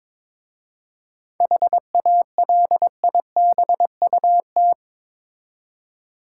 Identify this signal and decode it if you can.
Morse code, heard as a keyed tone, about 22 wpm, 700 Hz. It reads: HALIBUT
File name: Difficult Words with Hints - 22wpm